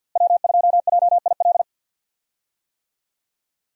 Morse code, heard as a keyed tone, about 52 wpm, 690 Hz. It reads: W2JIL
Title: USA Call Signs - 5 Characters - Repeated 3x - 50wpm